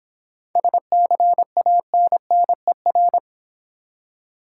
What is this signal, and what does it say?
Morse code, heard as a keyed tone, about 25 wpm, 690 Hz. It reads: SCANNER